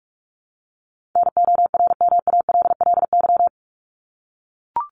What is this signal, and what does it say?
Morse code, heard as a keyed tone, about 44 wpm, 700 Hz. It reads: NORMALLY
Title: LY Suffix - Top 100 Words with Base Words 45wpm